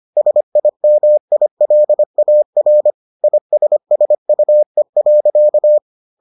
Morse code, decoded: SIMILAR ISSUE.

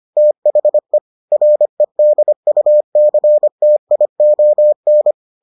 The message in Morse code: THE REDUCTION

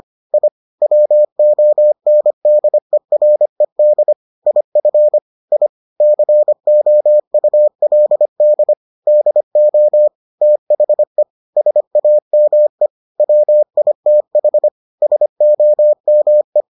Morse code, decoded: I WONDERED IF I COULD DO THE SAME WITH SOME

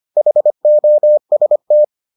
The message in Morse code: HOST

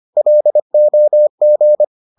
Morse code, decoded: LOG

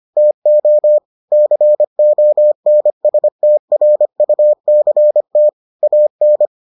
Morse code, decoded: TO CONSTRUCT AN

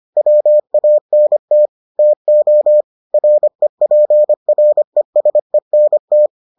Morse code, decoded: WANT TO REPRESENT

